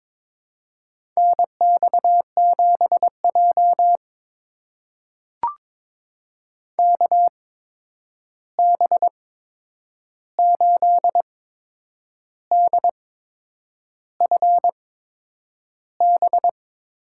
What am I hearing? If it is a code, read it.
Morse code, heard as a keyed tone, about 22 wpm, spaced as for 8 wpm, 700 Hz. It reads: NX7J KB8DFB
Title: USA Call Signs - 8x Character Spacing - 22wpm